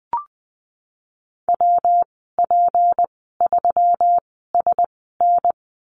Morse code, heard as a keyed tone, about 20 wpm, spaced as for 15 wpm, 700 Hz. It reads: WP3SN
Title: 154 Lesson 25g - Letter p - Call Signs 20wpm